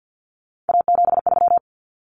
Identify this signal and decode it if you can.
Morse code, heard as a keyed tone, about 50 wpm, 700 Hz. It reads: A73
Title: Common Cut Numbers - 3-Digit 50wpm